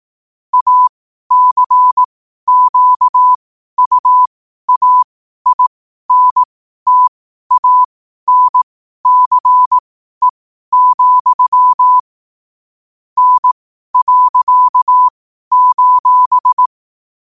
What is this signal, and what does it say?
Morse code, heard as a keyed tone, about 18 wpm, spaced as for 12 wpm, 1000 Hz. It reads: ACQUAINTANCE, N.:
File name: 12wpm